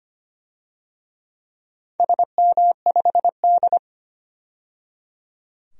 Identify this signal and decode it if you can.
Morse code, heard as a keyed tone, about 25 wpm, 700 Hz. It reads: SM5D
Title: DXCC-100 Call Signs - 4 Characters - 25wpm